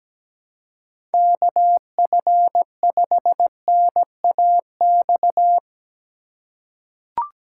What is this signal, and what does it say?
Morse code, heard as a keyed tone, about 17 wpm, 700 Hz. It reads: KF5NAX
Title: USA Call Signs - 6 Characters - 2x Character Spacing - 17wpm